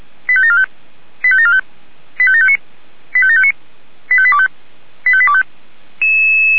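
Five-tone selective call (ZVEI), heard as quick runs of five tones, tones of about 70 ms. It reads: 85745 85745 85759 85759 85825 85825
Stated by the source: Five-tone ZVEI selective call received on an SDR.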